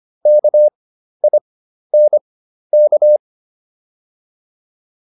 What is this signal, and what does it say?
Morse code, heard as a keyed tone, about 25 wpm, spaced as for 12 wpm, 600 Hz. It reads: KINK